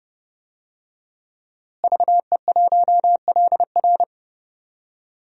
Morse code, heard as a keyed tone, about 30 wpm, 700 Hz. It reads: VE1LR
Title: DXCC-100 Call Signs - 5 Characters - Repeated 2x - 30wpm